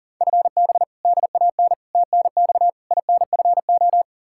Morse code, heard as a keyed tone, about 40 wpm, 700 Hz. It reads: FB DAN TNX INFO
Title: Practice QSO - Exchange 3 - More Info - 40wpm